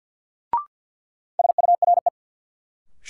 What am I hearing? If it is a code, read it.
Morse code, heard as a keyed tone, about 52 wpm, 700 Hz. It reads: SURE
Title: Sets of 3 Words - New Word Drill - 50wpm